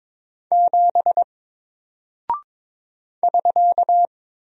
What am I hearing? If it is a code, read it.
Morse code, heard as a keyed tone, about 22 wpm, 700 Hz. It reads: 7 <SK>
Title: Single Character (Letter:Number:Punctuation:Prosign) 22wpm